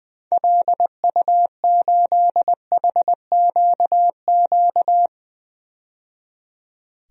Morse code, heard as a keyed tone, about 20 wpm, 700 Hz. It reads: LU8HQQ
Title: DXCC-100 Call Signs - 6 Characters - 20wpm